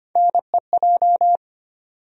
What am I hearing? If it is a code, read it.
Morse code, heard as a keyed tone, about 25 wpm, 700 Hz. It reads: NEJ
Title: USA Call Sign Postfixes - 3 Letters 25wpm